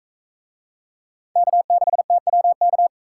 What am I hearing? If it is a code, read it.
Morse code, heard as a keyed tone, about 40 wpm, 700 Hz. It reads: K6TWK